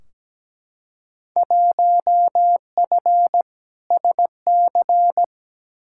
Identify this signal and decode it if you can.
Morse code, heard as a keyed tone, about 17 wpm, 700 Hz. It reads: 1F SC
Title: Field Day Exchanges Encore - 17wpm